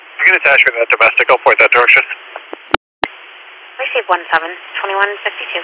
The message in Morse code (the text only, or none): none